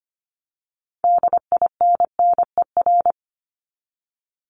Morse code, heard as a keyed tone, about 25 wpm, 700 Hz. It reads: DINNER
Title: Top 1751-2000 Words 25wpm